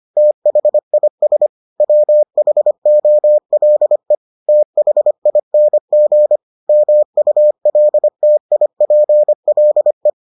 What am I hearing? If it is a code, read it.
Morse code, heard as a keyed tone, about 25 wpm, 600 Hz. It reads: THIS WHOLE THING MULTIPLE